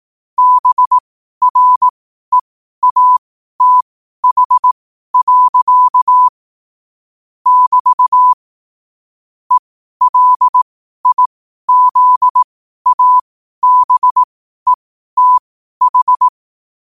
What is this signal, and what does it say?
Morse code, heard as a keyed tone, about 18 wpm, spaced as for 12 wpm, 1000 Hz. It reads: BREATH. <BT> ELIZABETH
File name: qotdfiles / 12wpm